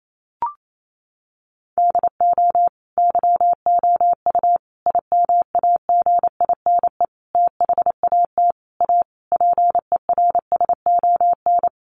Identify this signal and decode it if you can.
Morse code, heard as a keyed tone, about 28 wpm, 700 Hz. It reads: DO YOU IMAGINE THAT A PERSON
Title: Sentences from Top 1400 Words 28wpm